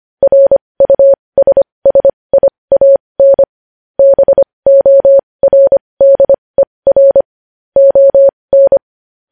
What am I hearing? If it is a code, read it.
Morse code, heard as a keyed tone, about 25 wpm, spaced as for 20 wpm, 550 Hz. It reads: RUSSIAN BORDER ON
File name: podcast / News headlines-1x-25x20